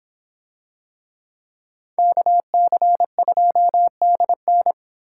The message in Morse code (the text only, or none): KC2DN